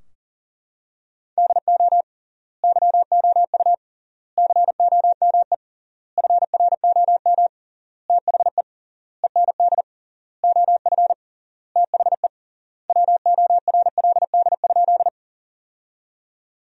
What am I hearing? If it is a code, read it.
Morse code, heard as a keyed tone, about 37 wpm, 700 Hz. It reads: DO YOU COME FROM THE END OF THE WORLD?
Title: 3x Word-Spacing - Sentences from Top 200 Words - 40wpm